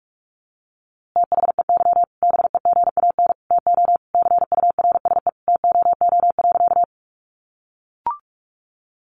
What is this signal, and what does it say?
Morse code, heard as a keyed tone, about 46 wpm, 700 Hz. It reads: THEY BEGAN TO CURSE TOO.
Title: A Sound Of Thunder 45wpm